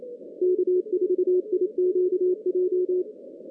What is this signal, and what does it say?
Morse code, heard as a keyed tone, about 28 wpm, 400 Hz. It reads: K4IQJ
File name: 2016 JAN CW NAQP / 2016 Jan naqp k5ot - 0-8